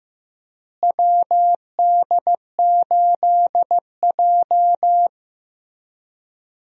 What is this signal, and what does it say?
Morse code, heard as a keyed tone, about 15 wpm, 700 Hz. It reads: WD8J
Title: USA Call Signs - 4 Characters - Repeated 3x - 15wpm